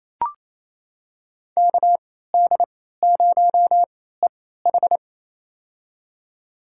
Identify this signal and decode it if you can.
Morse code, heard as a keyed tone, about 28 wpm, spaced as for 16 wpm, 700 Hz. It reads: KD0EH
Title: USA Call Signs - 3x Character Spacing - 28wpm